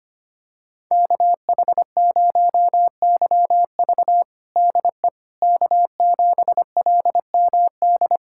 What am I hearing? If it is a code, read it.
Morse code, heard as a keyed tone, about 25 wpm, 700 Hz. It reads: KH0YV DE K7LMD
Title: USA Callsign DE Callsign - 5 Characters - 25wpm